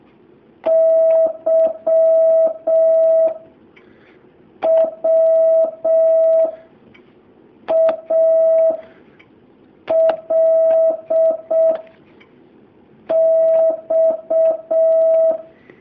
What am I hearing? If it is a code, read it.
Morse code, heard as a keyed tone, about 6 wpm, 650 Hz. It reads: Y W A L X
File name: jh8cbh / 2019cw